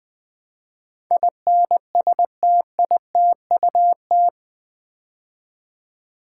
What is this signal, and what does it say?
Morse code, heard as a keyed tone, about 20 wpm, 700 Hz. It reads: INSTITUT
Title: Build A Word - 4 Letters At A Time - 11-Letter Words - 20wpm